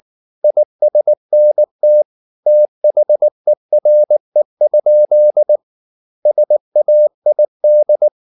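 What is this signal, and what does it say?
Morse code, heard as a keyed tone, about 19 wpm, 600 Hz. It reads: ISNT THERE? SAID
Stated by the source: Morse practice at 18 wpm, English text, from CW med Gnister.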